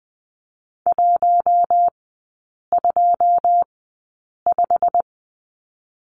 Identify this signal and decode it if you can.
Morse code, heard as a keyed tone, about 20 wpm, spaced as for 8 wpm, 690 Hz. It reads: 125
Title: 087 Lesson 15e - Number 5 - Three Characters 20wpm